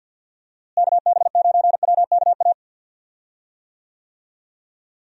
Morse code, heard as a keyed tone, about 50 wpm, 700 Hz. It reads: KB9WKA
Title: USA Call Signs - 6 Characters - Repeated 3x - 50wpm